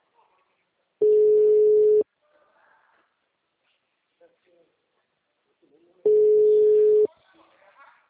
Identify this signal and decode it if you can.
Ringback tone (Europe, 425 Hz), with speech.